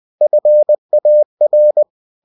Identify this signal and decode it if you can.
Morse code, heard as a keyed tone, about 20 wpm, 600 Hz. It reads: FAR
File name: cw-books / winnie-the-pooh / winnie-20wpm-page-0000